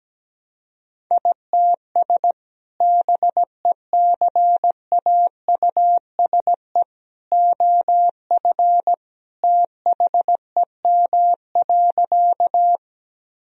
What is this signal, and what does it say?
Morse code, heard as a keyed tone, about 17 wpm, 700 Hz. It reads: ITS BECAUSE OF THEM.
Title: Sentences from Top 1000 Words - Review Part 3 - 17wpm